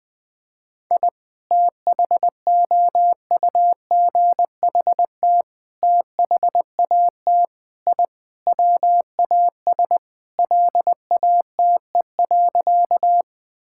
Morse code, heard as a keyed tone, about 20 wpm, 700 Hz. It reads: I THOUGHT THAT I WAS LATE.
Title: Sentences with 3-Word Phrases - 20wpm